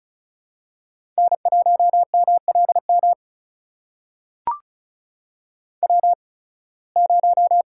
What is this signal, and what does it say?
Morse code, heard as a keyed tone, about 34 wpm, spaced as for 23 wpm, 700 Hz. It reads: N1MLM W0
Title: USA Call Signs - 5 Characters - 8x Character Spacing - 35wpm